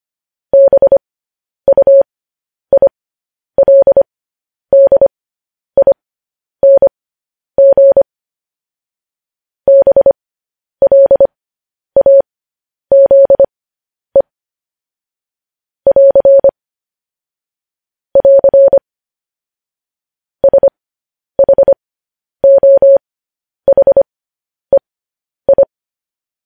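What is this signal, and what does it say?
Morse code, heard as a keyed tone, about 25 wpm, spaced as for 10 wpm, 550 Hz. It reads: BUILDING BLAZE <AR> <AR> SHOHEI
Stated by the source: Morse news headlines at 10.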